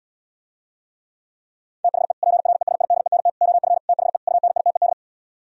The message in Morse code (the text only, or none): THE PRESIDENT OF THE UNITED